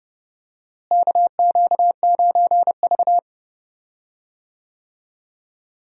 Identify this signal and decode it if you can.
Morse code, heard as a keyed tone, about 30 wpm, 700 Hz. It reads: KQ9V
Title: USA Call Signs - 4 Characters - Repeated 3x - 30wpm